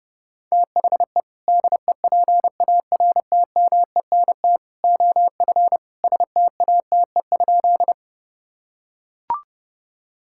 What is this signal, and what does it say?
Morse code, heard as a keyed tone, about 30 wpm, 700 Hz. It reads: THE DEPARTMENT OF STATE?
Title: Sentences from Top 1000 Words - Review Part 5 - 30wpm